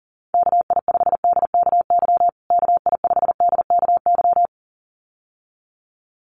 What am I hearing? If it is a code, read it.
Morse code, heard as a keyed tone, about 40 wpm, 700 Hz. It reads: KI5DKY KI5DKY
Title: USA Call Signs Repeated 2x 40wpm